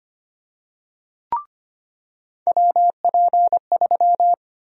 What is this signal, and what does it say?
Morse code, heard as a keyed tone, about 25 wpm, 700 Hz. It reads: WP3
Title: USA Call Sign Prefixes - 25wpm